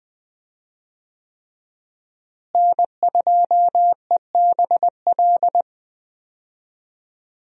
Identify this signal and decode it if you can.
Morse code, heard as a keyed tone, about 21 wpm, 700 Hz. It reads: N2EBL
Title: USA Call Signs - 5 Characters - Repeated 3x - 20wpm